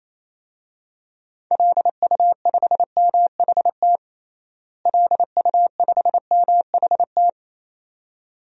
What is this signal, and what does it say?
Morse code, heard as a keyed tone, about 28 wpm, 700 Hz. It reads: LU5MHT LU5MHT
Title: DXCC-100 Call Signs - 6 Characters - Repeated 2x - 28wpm